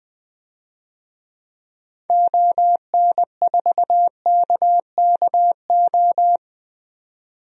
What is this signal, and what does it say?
Morse code, heard as a keyed tone, about 20 wpm, 700 Hz. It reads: ON4KKO
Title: DXCC-100 Call Signs - 6 Characters - 20wpm